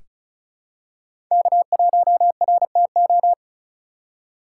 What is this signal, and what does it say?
Morse code, heard as a keyed tone, about 34 wpm, 700 Hz. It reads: K1RTO
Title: USA Call Signs - 5 Characters - 4x Character Spacing - 35wpm